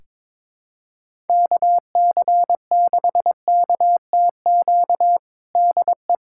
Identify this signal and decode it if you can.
Morse code, heard as a keyed tone, about 22 wpm, 700 Hz. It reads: KC6KTQ DE
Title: USA Callsign DE Callsign 22wpm